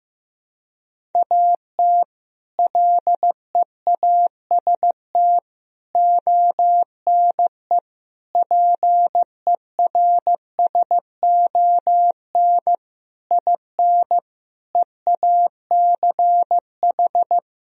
Morse code, heard as a keyed tone, about 15 wpm, 700 Hz. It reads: AT LEAST ONE PERSON IN EACH
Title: Sentences from Top 1000 Words - Review Part 1 - 15wpm